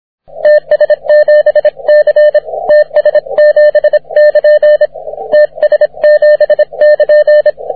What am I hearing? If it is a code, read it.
Morse code, heard as a keyed tone, about 25 wpm, 600 Hz, with radio noise. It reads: TS7C TS7<KN> TS7<KN>
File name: wp / ts7c 30m cw